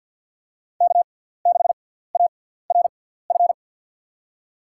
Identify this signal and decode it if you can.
Morse code, heard as a keyed tone, about 53 wpm, spaced as for 17 wpm, 700 Hz. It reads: K6ARF
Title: USA Call Signs - 5 Characters - 6x Character Spacing - 50wpm